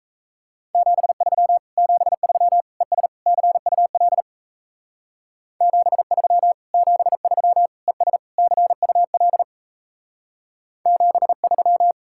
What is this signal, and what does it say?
Morse code, heard as a keyed tone, about 35 wpm, 700 Hz. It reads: 73 73 ES CUL 73 73 ES CUL 73
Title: Speed-Racing Common QSO Phrases - 28wpm